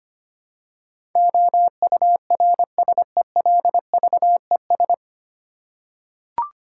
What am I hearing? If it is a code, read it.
Morse code, heard as a keyed tone, about 25 wpm, 700 Hz. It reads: OURSELVES
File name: Nine Letter Words - 25wpm